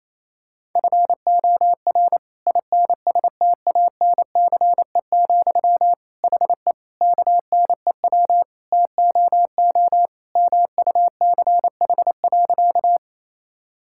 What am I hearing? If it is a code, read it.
Morse code, heard as a keyed tone, about 27 wpm, 700 Hz. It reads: FOR INSTANCE, HE KNEW TOO MUCH.
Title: Sentences from Top 1000 Words - Review Part 5 - 28wpm